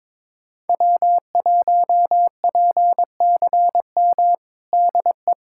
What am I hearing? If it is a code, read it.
Morse code, heard as a keyed tone, about 22 wpm, 700 Hz. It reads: W1PCM DE
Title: USA Callsign DE Callsign - 5 Characters - 22wpm